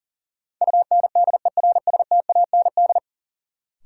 Morse code, heard as a keyed tone, about 40 wpm, 700 Hz. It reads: UNDERSTAND
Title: Sentences from Top 600 Words - New Word Drill - 40wpm